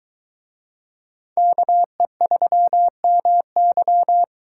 Morse code, heard as a keyed tone, about 22 wpm, 700 Hz. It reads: KE3MY